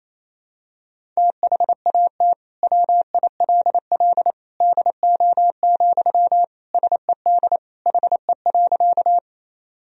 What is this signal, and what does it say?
Morse code, heard as a keyed tone, about 28 wpm, 700 Hz. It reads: THAT WILL DO, SED HE.
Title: Sentences from Top 100 Words and Abbreviations 28wpm